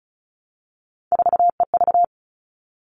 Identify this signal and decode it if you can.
Morse code, heard as a keyed tone, about 34 wpm, 700 Hz. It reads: 4EV
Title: All Cut Numbers - 3-Digit 35wpm